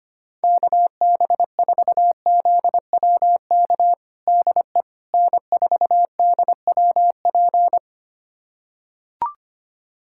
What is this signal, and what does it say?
Morse code, heard as a keyed tone, about 25 wpm, 700 Hz. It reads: KB4ZWK DE N4DWP
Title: USA Callsign DE Callsign 25wpm